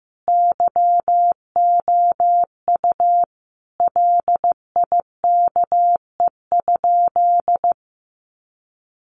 Morse code, heard as a keyed tone, about 14 wpm, 700 Hz. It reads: YOU LIKE?
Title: Sentences from Top 1200 Words 15wpm